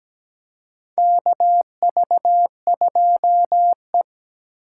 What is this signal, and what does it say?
Morse code, heard as a keyed tone, about 17 wpm, 700 Hz. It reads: KV2E
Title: USA Call Signs - 4 Characters - 3x Character Spacing - 17wpm